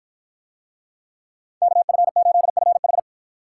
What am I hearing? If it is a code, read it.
Morse code, heard as a keyed tone, about 47 wpm, 700 Hz. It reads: KF8FH